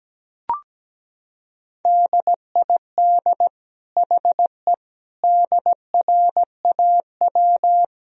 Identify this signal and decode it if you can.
Morse code, heard as a keyed tone, about 17 wpm, 700 Hz. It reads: DID HE DRAW
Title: Sentences from Top 1000 Words - Review Part 2 - 17wpm